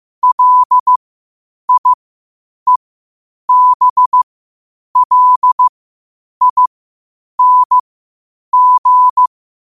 Morse code, heard as a keyed tone, about 15 wpm, spaced as for 8 wpm, 1000 Hz. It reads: LIEBLING
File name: qotdfiles / 8wpm